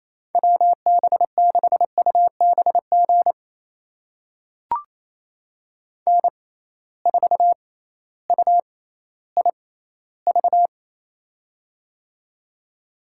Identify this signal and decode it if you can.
Morse code, heard as a keyed tone, about 28 wpm, spaced as for 15 wpm, 700 Hz. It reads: WB6UBG N4UIV
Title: USA Call Signs - 6x Character Spacing - 28wpm